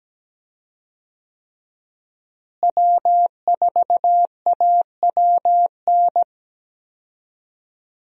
Morse code, heard as a keyed tone, about 17 wpm, 700 Hz. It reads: W4AWN
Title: USA Call Signs - 5 Characters - Repeated 3x - 17wpm